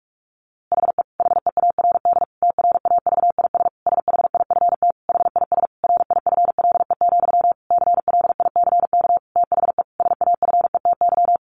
Morse code, heard as a keyed tone, about 45 wpm, 700 Hz. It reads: HE HEARD TRAVIS SHIFT HIS RIFLE, CLICK THE SAFETY
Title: A Sound Of Thunder 45wpm